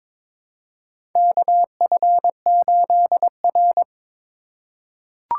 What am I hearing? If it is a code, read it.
Morse code, heard as a keyed tone, about 23 wpm, 690 Hz. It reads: KF8R
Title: USA Call Signs - 4 Characters - 3x Character Spacing - 22wpm